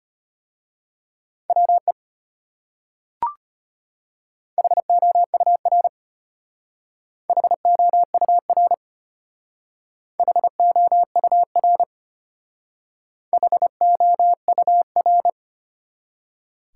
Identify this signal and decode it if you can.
Morse code, heard as a keyed tone, about 31 wpm, 700 Hz. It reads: WE HOUR HOUR HOUR HOUR
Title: Speed-Racing Top 180 QSO Elements 25wpm